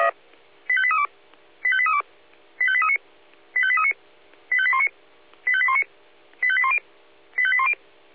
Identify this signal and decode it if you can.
Five-tone selective call (ZVEI), heard as quick runs of five tones, tones of about 70 ms. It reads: 85832 85832 85839 85839 85819 85819 85810 85810
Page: SDR/ZVEI/85278